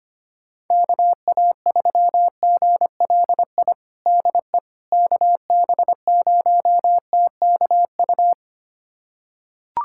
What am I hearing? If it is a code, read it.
Morse code, heard as a keyed tone, about 25 wpm, 700 Hz. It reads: KA3GLI DE KB0TKU
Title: USA Callsign DE Callsign - 6 Characters - 25wpm